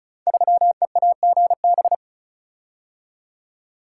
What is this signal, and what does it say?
Morse code, heard as a keyed tone, about 36 wpm, 700 Hz. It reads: 3EAGB
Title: Code Groups - 5 Letter:Numbers - 35wpm